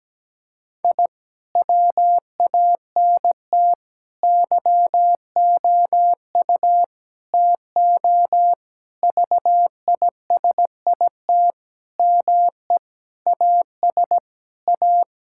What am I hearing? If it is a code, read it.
Morse code, heard as a keyed tone, about 17 wpm, 700 Hz. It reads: I WANT YOU TO VISIT ME AS A
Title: Sentences from Top 1000 Words - Review Part 6 - 17wpm